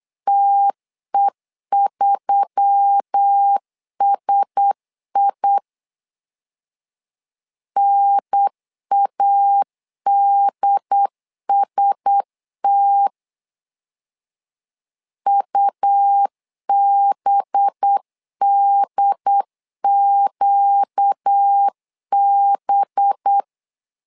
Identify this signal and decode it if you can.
Morse code, heard as a keyed tone, about 8 wpm, 800 Hz. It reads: TE3SI NADST UBDQB
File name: LEZIONI / 5 LBVC / 5 LB 8wpm 2816